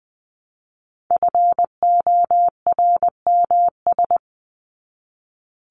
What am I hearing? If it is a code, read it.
Morse code, heard as a keyed tone, about 20 wpm, 700 Hz. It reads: FORMS
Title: ES:S Suffix - Top 100 Words with Base Words 20wpm